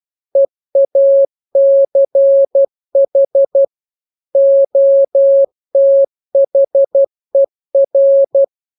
Morse code, heard as a keyed tone, about 12 wpm, 550 Hz. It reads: EACH OTHER
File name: jokes / joke 17 12 0000